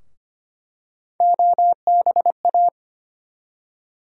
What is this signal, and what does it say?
Morse code, heard as a keyed tone, about 25 wpm, 700 Hz. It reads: OBA
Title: USA Call Sign Postfixes - 3 Letters 25wpm